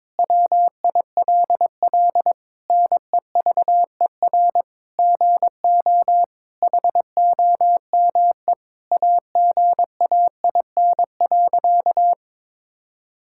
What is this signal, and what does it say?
Morse code, heard as a keyed tone, about 22 wpm, 700 Hz. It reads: WILL NEVER GO HOME AGAIN.
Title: Sentences from Top 1000 Words - Review Part 6 - 22wpm